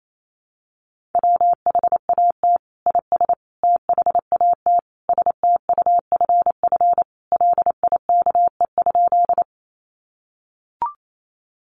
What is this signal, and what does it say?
Morse code, heard as a keyed tone, about 28 wpm, 700 Hz. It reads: WHAT IS THAT STUFF LIKE?
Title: Sentences from Top 1400 Words 28wpm